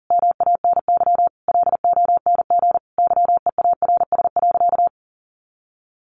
Morse code, heard as a keyed tone, about 40 wpm, 700 Hz. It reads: MANY LONG YEARS.
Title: Sentences from Top 200 Words 40wpm